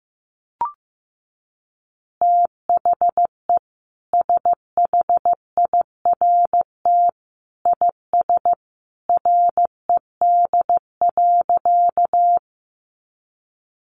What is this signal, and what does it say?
Morse code, heard as a keyed tone, about 15 wpm, 700 Hz. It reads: THE SHIRT IS RED.
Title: Additional Easy Phrases 15wpm